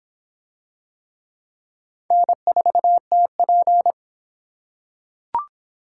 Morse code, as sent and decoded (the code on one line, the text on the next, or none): -. ....- - .--.
N4TP